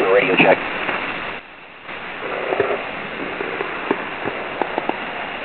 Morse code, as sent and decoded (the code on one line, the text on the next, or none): none